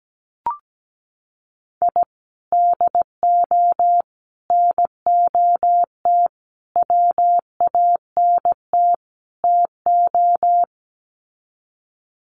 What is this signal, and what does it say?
Morse code, heard as a keyed tone, about 17 wpm, 700 Hz. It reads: I DO NOT WANT TO
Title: Sets of 5 Words 17wpm